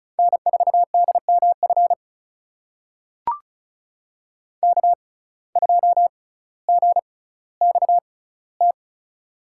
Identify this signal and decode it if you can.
Morse code, heard as a keyed tone, about 36 wpm, spaced as for 18 wpm, 700 Hz. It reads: N4DMF K2GXT
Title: USA Call Signs - 6x Character Spacing - 35wpm